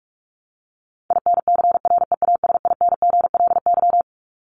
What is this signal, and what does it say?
Morse code, heard as a keyed tone, about 44 wpm, 700 Hz. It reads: INCREASINGLY
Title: LY Suffix - Top 100 Words with Base Words 45wpm